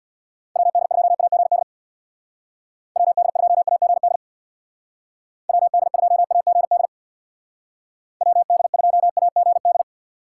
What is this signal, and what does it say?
Morse code, heard as a keyed tone, about 65 wpm, 700 Hz. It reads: WD2AGD WD2AGD WD2AGD WD2AGD